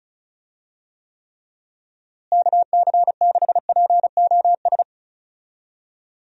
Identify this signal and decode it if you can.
Morse code, heard as a keyed tone, about 34 wpm, 700 Hz. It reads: KC6POS
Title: USA Call Signs - 6 Characters - Repeated 2x - 35wpm